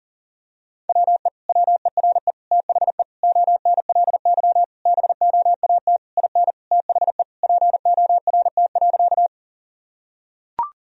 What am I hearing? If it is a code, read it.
Morse code, heard as a keyed tone, about 40 wpm, 700 Hz. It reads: WE WERE THE ONLY BOAT IN THE PORT.
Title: Sentences from Top 1000 Words - Review Part 1 - 40wpm